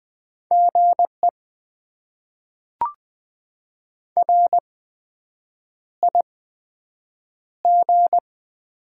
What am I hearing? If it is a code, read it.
Morse code, heard as a keyed tone, about 20 wpm, spaced as for 7 wpm, 700 Hz. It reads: GE RIG
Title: Top 20 QSO Elements - 8x Character Spacing - 20wpm